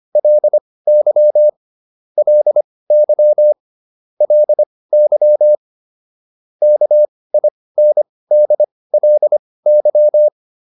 Morse code, heard as a keyed tone, about 25 wpm, spaced as for 18 wpm, 600 Hz. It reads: LY LY LY KINDLY